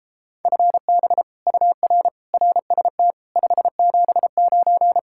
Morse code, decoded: FB UR RST 579

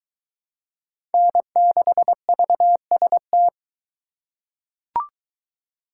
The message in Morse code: N6VST

